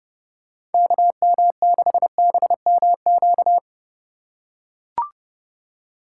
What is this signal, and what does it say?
Morse code, heard as a keyed tone, about 30 wpm, 700 Hz. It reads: KM6BMQ
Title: USA Call Signs - 6 Characters - 3x Character Spacing - 30wpm